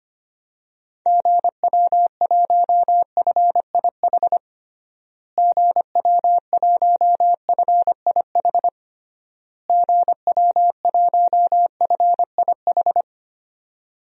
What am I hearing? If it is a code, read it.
Morse code, heard as a keyed tone, about 25 wpm, 700 Hz. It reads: GW1FIH GW1FIH GW1FIH
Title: DXCC-100 Call Signs - 6 Characters - Repeated 3x - 25wpm